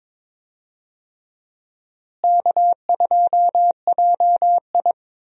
Morse code, heard as a keyed tone, about 22 wpm, 700 Hz. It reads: K2JI
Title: USA Call Signs - 4 Characters - Repeated 3x - 22wpm